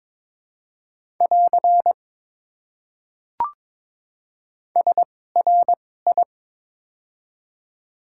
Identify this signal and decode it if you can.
Morse code, heard as a keyed tone, about 21 wpm, spaced as for 16 wpm, 700 Hz. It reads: <AR> SRI
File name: Top 90 QSO Elements - 2x Character Spacing - 22wpm